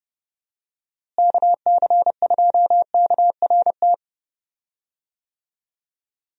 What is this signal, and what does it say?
Morse code, heard as a keyed tone, about 30 wpm, 700 Hz. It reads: KC2KRT
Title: USA Call Signs - 6 Characters - Repeated 2x - 30wpm